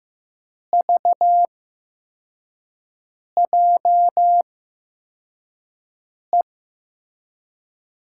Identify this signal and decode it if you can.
Morse code, heard as a keyed tone, about 15 wpm, spaced as for 4 wpm, 700 Hz. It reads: VJE
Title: USA Call Signs - 6 Characters - 8x Character Spacing - 15wpm